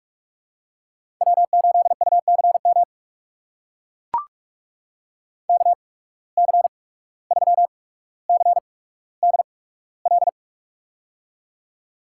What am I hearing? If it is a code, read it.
Morse code, heard as a keyed tone, about 44 wpm, spaced as for 18 wpm, 700 Hz. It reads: W8UCM KC3CDL
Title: USA Call Signs - 8x Character Spacing - 45wpm